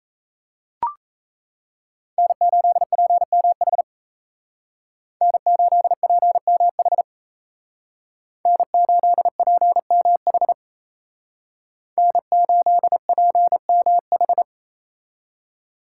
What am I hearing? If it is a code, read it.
Morse code, heard as a keyed tone, about 37 wpm, 700 Hz. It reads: N8PMH N8PMH N8PMH N8PMH